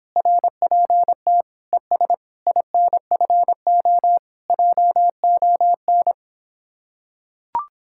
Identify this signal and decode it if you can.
Morse code, heard as a keyed tone, about 26 wpm, 700 Hz. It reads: RPT ES INFO JON